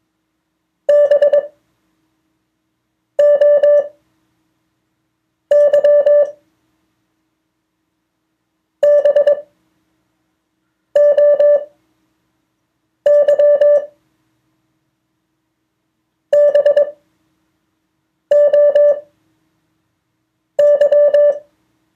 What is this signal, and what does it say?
Morse code, heard as a keyed tone, about 21 wpm, spaced as for 5 wpm, 580 Hz. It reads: BOY BOY BOY
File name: 2of4-words-1